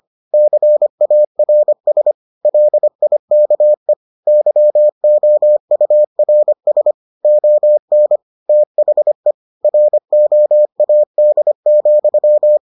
Morse code, heard as a keyed tone, about 25 wpm, 600 Hz. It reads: CARS LIKE YOURS ON THE ROAD,